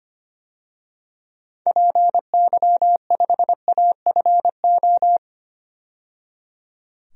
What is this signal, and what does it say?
Morse code, heard as a keyed tone, about 25 wpm, 700 Hz. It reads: PY5AFO
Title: DXCC-100 Call Signs - 6 Characters - 25wpm